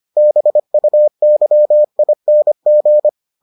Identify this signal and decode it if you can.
Morse code, heard as a keyed tone, about 25 wpm, 600 Hz. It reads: BUYING